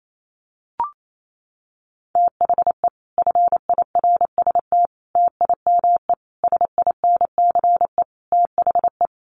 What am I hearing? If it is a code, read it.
Morse code, heard as a keyed tone, about 28 wpm, 700 Hz. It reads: THE FIRST TIME SINCE THE
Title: Sets of 5 Words Encore 28wpm